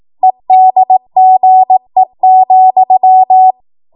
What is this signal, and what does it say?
Morse code, heard as a keyed tone, about 18 wpm, 750 Hz. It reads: EDGE,
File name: Morse / 18WPM